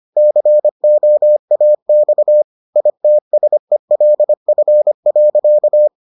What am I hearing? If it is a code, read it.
Morse code, heard as a keyed tone, about 25 wpm, 600 Hz. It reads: COAX ITSELF.